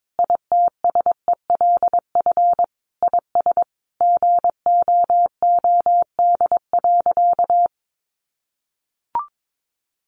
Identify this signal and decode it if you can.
Morse code, heard as a keyed tone, about 22 wpm, 700 Hz. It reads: ITSELF IS GOOD.
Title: Sentences from Top 800 Words 22wpm